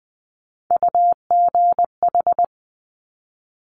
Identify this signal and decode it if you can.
Morse code, heard as a keyed tone, about 20 wpm, 700 Hz. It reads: UGH
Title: Top Three Letter Combinations in Words 20wpm